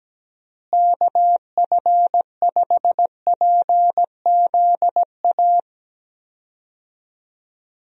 Morse code, heard as a keyed tone, about 17 wpm, 700 Hz. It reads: KF5PZA
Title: USA Call Signs - 6 Characters - Repeated 3x - 17wpm